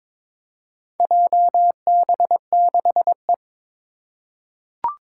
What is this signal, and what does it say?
Morse code, heard as a keyed tone, about 22 wpm, 700 Hz. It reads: JB6E
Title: Four Challenging Letter-Number Combinations 22wpm